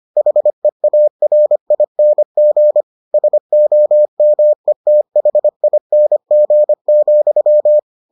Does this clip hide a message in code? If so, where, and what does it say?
Yes: Morse code from 0.2 to 7.8 s, heard as a keyed tone, about 25 wpm, 600 Hz. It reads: HEARING SOMETHING,